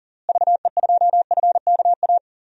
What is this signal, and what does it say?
Morse code, heard as a keyed tone, about 40 wpm, 700 Hz. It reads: VE2FKA
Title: DXCC-100 Call Signs - 6 Characters - Repeated 3x - 40wpm